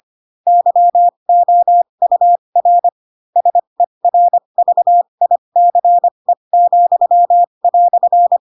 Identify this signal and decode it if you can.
Morse code, heard as a keyed tone, about 25 wpm, spaced as for 22 wpm, 700 Hz. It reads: YOUR SERVICE,"
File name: Pinocchio / Pinocchio ch2 18 25 ik1hge